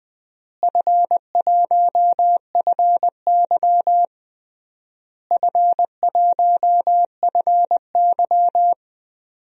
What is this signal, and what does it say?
Morse code, heard as a keyed tone, about 20 wpm, 700 Hz. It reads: F1FY F1FY
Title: DXCC-100 Call Signs - 4 Characters - Repeated 3x - 20wpm